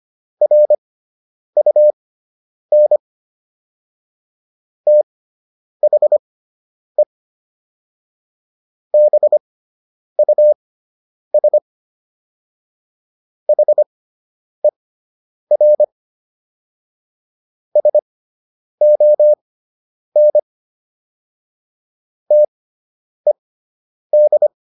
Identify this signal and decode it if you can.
Morse code, heard as a keyed tone, about 25 wpm, spaced as for 9 wpm, 600 Hz. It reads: RUN THE BUS HER SON TED